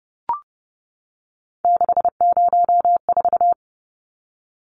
Morse code, heard as a keyed tone, about 30 wpm, 700 Hz. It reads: 604
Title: Numbers - 3-Digit 30wpm